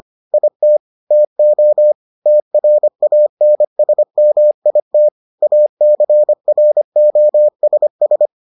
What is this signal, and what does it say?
Morse code, heard as a keyed tone, about 25 wpm, 600 Hz. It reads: IT TO TRANSMIT ACROSS